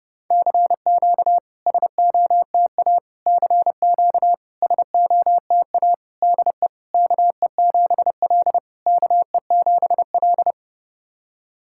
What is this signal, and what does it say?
Morse code, heard as a keyed tone, about 30 wpm, 690 Hz. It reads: CQ SOTA CQ SOTA DE KE7L KE7L
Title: SOTA QSO Practice 30wpm